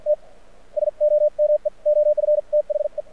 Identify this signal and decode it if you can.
Morse code, heard as a keyed tone, about 53 wpm, spaced as for 46 wpm, 600 Hz. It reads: T SOME OF THE
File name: archived / CRSnet